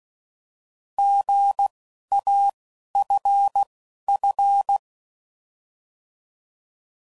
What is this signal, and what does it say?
Morse code, heard as a keyed tone, about 16 wpm, spaced as for 11 wpm, 800 Hz. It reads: GAFF